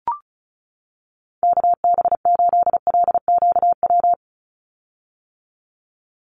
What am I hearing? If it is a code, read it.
Morse code, heard as a keyed tone, about 34 wpm, 700 Hz. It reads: KB8LQW
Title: USA Call Signs 35wpm